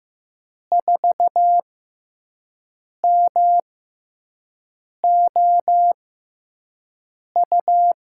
Morse code, heard as a keyed tone, about 15 wpm, spaced as for 5 wpm, 700 Hz. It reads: 4MOU
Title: USA Call Signs - 6 Characters - 6x Character Spacing - 15wpm